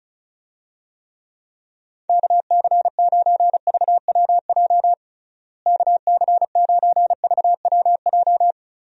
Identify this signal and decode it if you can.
Morse code, heard as a keyed tone, about 34 wpm, 700 Hz. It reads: KC9VWJ KC9VWJ
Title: USA Call Signs - 6 Characters - Repeated 3x - 35wpm